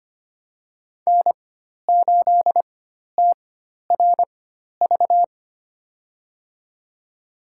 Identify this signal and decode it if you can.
Morse code, heard as a keyed tone, about 25 wpm, spaced as for 12 wpm, 700 Hz. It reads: N8TRV
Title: USA Call Signs - 5 Characters - 4x Character Spacing - 25wpm